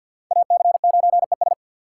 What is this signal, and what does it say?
Morse code, heard as a keyed tone, about 46 wpm, 700 Hz. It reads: AC9ES
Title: USA Call Signs - 4x Character Spacing - 50wpm